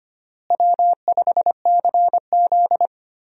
Morse code, heard as a keyed tone, about 25 wpm, 700 Hz. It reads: W5CZ
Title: USA Call Signs - 4 Characters - Repeated 2x - 25wpm